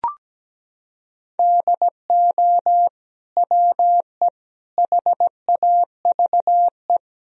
Morse code, heard as a keyed tone, about 17 wpm, 700 Hz. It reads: DO WE HAVE
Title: Sentences from Top 1000 Words - Review Part 3 - 17wpm